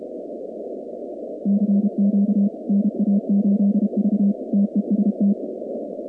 Morse code, heard as a keyed tone, about 32 wpm, 200 Hz. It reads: CQ NA8V TEST
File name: contests / 2019-cw-wpx-kt5j-k5pi- -2330 0-47